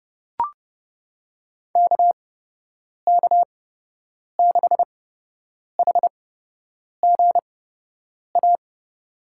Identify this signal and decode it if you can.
Morse code, heard as a keyed tone, about 30 wpm, spaced as for 8 wpm, 700 Hz. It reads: KK6HGA
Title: USA Call Signs - 6 Characters - 8x Character Spacing - 30wpm